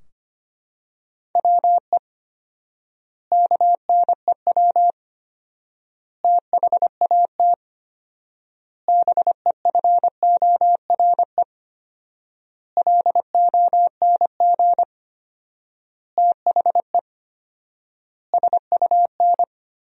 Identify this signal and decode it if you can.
Morse code, heard as a keyed tone, about 25 wpm, 700 Hz. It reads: WE KNEW THAT BEFORE LONG THE SUN
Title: 4x Word-Spacing - Sentences from Top 400 Words - 25wpm